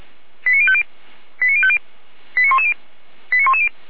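Five-tone selective call (ZVEI), heard as quick runs of five tones, tones of about 70 ms. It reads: 89050 89050 89110 89110
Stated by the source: Five-tone ZVEI selective call 89174 received by SDR.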